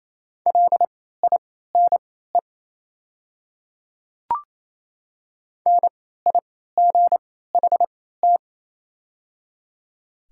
Morse code, heard as a keyed tone, about 28 wpm, spaced as for 16 wpm, 700 Hz. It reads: LINE NIGHT
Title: 3x Character Spacing - Top 500 Words - 28wpm